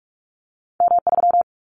Morse code, heard as a keyed tone, about 44 wpm, 700 Hz. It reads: M3
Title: Callsign Prefixes for Top 25 DXCC - Repeated 3x - 45wpm